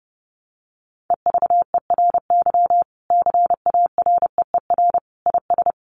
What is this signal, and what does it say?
Morse code, heard as a keyed tone, about 30 wpm, 700 Hz. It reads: EVERY CAREER IS
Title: Sentences from Top 1000 Words 30wpm